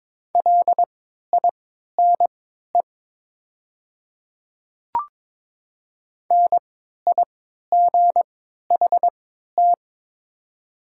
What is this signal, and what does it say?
Morse code, heard as a keyed tone, about 22 wpm, spaced as for 12 wpm, 700 Hz. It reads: LINE NIGHT
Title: 3x Character Spacing - Top 500 Words - 22wpm